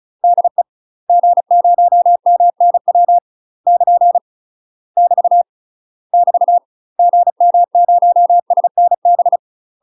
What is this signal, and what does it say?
Morse code, heard as a keyed tone, about 35 wpm, 700 Hz. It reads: DE G0MNW <KN> <BT> <BT> GM0SNB
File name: UK MASTER QSOs TRAINING / 35WPM / UK MASTER QSOs 0013